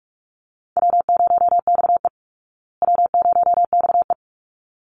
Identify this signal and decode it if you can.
Morse code, heard as a keyed tone, about 44 wpm, 700 Hz. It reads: W0XE W0XE
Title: Easy Call Signs Repeated 2x - 45wpm